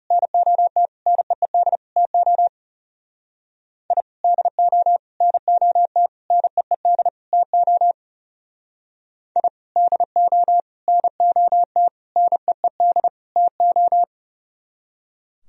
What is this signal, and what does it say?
Morse code, heard as a keyed tone, about 38 wpm, 700 Hz. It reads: NOT NEED TO I DO NOT NEED TO I DO NOT NEED TO